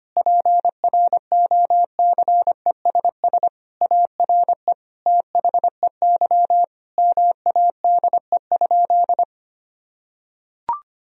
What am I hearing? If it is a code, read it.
Morse code, heard as a keyed tone, about 25 wpm, 700 Hz. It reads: PROCESS ARE THEY MADE?
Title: Sentences from Top 1000 Words - Review Part 3 - 25wpm